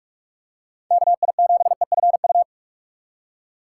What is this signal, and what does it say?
Morse code, heard as a keyed tone, about 46 wpm, 700 Hz. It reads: KI7EFU
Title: USA Call Signs - 6x Character Spacing - 45wpm